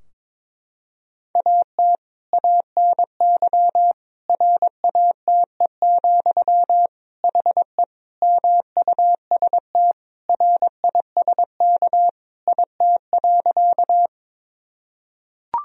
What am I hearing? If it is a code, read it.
Morse code, heard as a keyed tone, about 22 wpm, 700 Hz. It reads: AT ANY RATE, HE MUST RISK IT.
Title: Sentences from Top 1000 Words - Review Part 4 - 22wpm